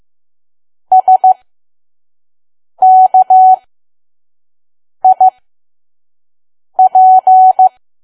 Morse code, heard as a keyed tone, about 15 wpm, spaced as for 5 wpm, 750 Hz. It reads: SKIP
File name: Archive / 5 WPM files / PM